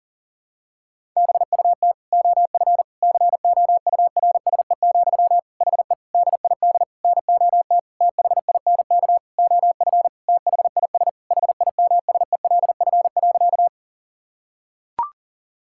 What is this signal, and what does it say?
Morse code, heard as a keyed tone, about 40 wpm, 700 Hz. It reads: BUT OF COURSE, HE DID NOT THINK OF THIS HIMSELF.
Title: Sentences from Top 1600 Words 40wpm